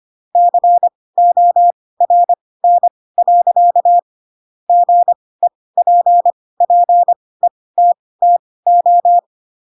Morse code, heard as a keyed tone, about 25 wpm, spaced as for 18 wpm, 700 Hz. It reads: CORN. GEPPETTO
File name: Pinocchio / Pinocchio ch2 12 25 ik1hge